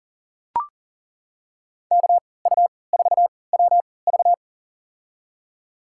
Morse code, heard as a keyed tone, about 39 wpm, spaced as for 22 wpm, 700 Hz. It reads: KU4WV
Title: USA Call Signs - 5 Characters - 3x Character Spacing - 40wpm